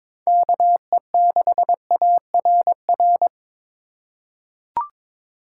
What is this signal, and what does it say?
Morse code, heard as a keyed tone, about 22 wpm, 700 Hz. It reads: KE6ARR
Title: USA Call Signs - 6 Characters - 3x Character Spacing - 22wpm